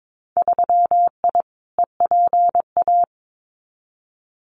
Morse code, heard as a keyed tone, about 22 wpm, 700 Hz. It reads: 3I EPA
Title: Winter Field Day Exchanges 22wpm